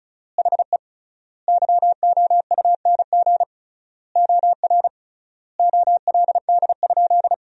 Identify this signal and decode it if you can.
Morse code, heard as a keyed tone, about 36 wpm, 700 Hz. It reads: HE YOUNG OR OLD?
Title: 3x Word-Spacing - Sentences from Top 400 Words - 35wpm